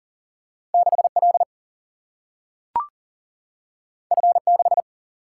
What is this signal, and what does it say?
Morse code, heard as a keyed tone, about 40 wpm, 700 Hz. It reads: 6L F6
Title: Single-Letter Single-Number 40wpm